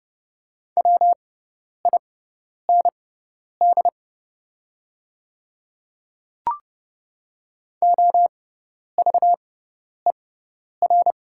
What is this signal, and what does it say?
Morse code, heard as a keyed tone, about 29 wpm, spaced as for 10 wpm, 700 Hz. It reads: WIND OVER
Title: 6x Character Spacing - Top 500 Words - 30wpm